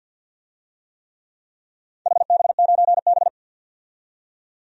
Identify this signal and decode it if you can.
Morse code, heard as a keyed tone, about 51 wpm, 700 Hz. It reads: HB9B
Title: DXCC-100 Call Signs - 4 Characters - 50wpm